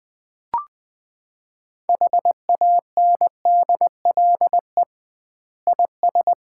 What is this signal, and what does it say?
Morse code, heard as a keyed tone, about 20 wpm, 700 Hz. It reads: HANDLE IS
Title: Common QSO Phrases - 2x Word-Spacing - 20wpm